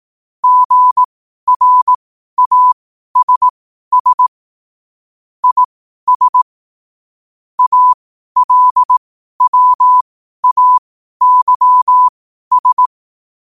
Morse code, heard as a keyed tone, about 18 wpm, spaced as for 12 wpm, 1000 Hz. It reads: GRASS IS ALWAYS
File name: qotdfiles / 12wpm